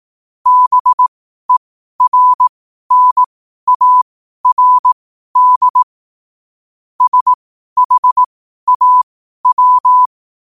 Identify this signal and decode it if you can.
Morse code, heard as a keyed tone, about 18 wpm, spaced as for 12 wpm, 1000 Hz. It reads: BERNARD SHAW
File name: qotdfiles / 12wpm